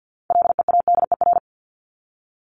Morse code, heard as a keyed tone, about 48 wpm, 700 Hz. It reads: LEADER
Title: Top 1501-1750 Words 50wpm